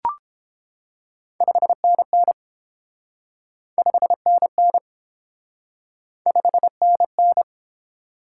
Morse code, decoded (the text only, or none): 5NN 5NN 5NN